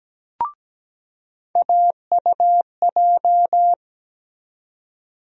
Morse code, heard as a keyed tone, about 17 wpm, 700 Hz. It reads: AUJ
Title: USA Call Sign Postfixes - 3 Letters 17wpm